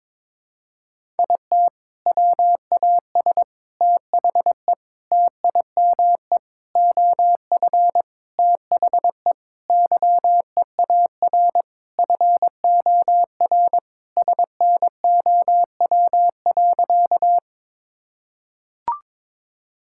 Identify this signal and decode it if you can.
Morse code, heard as a keyed tone, about 22 wpm, 700 Hz. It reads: IT WAS THE TIME OF THE YEAR FOR SNOW.
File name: practice / 22wpm / Sentences with 4-Word Phrases - 22wpm